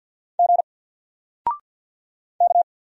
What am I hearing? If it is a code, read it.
Morse code, heard as a keyed tone, about 50 wpm, 700 Hz. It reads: G K
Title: Single Letter-Number 50wpm